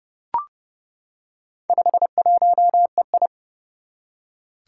Morse code, heard as a keyed tone, about 30 wpm, 700 Hz. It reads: 51EI
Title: Four Challenging Letter-Number Combinations 30wpm